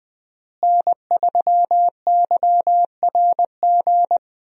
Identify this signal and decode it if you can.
Morse code, heard as a keyed tone, about 20 wpm, 700 Hz. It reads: N3YRG
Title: USA Call Signs - 5 Characters - Repeated 3x - 20wpm